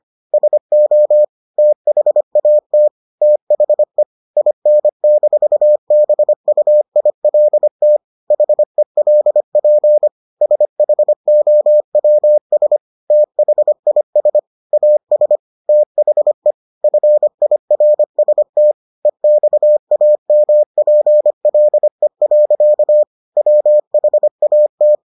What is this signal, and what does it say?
Morse code, heard as a keyed tone, about 25 wpm, 600 Hz. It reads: SO THAT THE IN-BUILT HELP SHOWS THIS AS THE FIRST EXAMPLE. WHAT